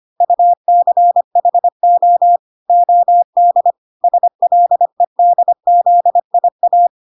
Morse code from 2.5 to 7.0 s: OD SLEDZIA